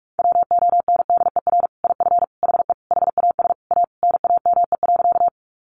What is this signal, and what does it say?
Morse code, heard as a keyed tone, about 44 wpm, 700 Hz. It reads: WONDER IF HE HAS A NAME.
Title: Sentences from Top 500 Words 45wpm